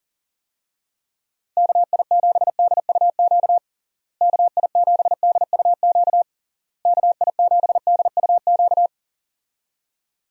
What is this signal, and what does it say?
Morse code, heard as a keyed tone, about 40 wpm, 700 Hz. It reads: KI7DUQ KI7DUQ KI7DUQ
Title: USA Call Signs - 6 Characters - Repeated 3x - 40wpm